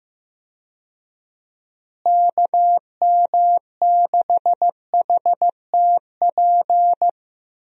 Morse code, heard as a keyed tone, about 16 wpm, 700 Hz. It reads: KM6HTP
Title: USA Call Signs - 6 Characters - Repeated 2x - 15wpm